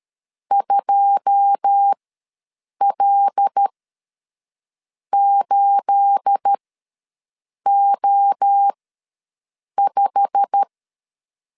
Morse code, heard as a keyed tone, about 13 wpm, spaced as for 6 wpm, 800 Hz. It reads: 2L8O5
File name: LEZIONI / 10 SIGNS / 10B INT ws 12wpm 3527